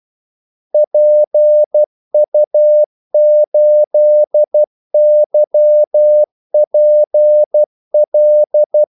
Morse code, heard as a keyed tone, about 12 wpm, 600 Hz. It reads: PU8YPL
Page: Kaldesignaler 12 wpm